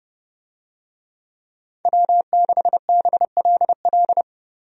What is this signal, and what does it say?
Morse code, heard as a keyed tone, about 32 wpm, 700 Hz. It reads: W6BLL
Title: USA Call Signs - 5 Characters - Repeated 2x - 30wpm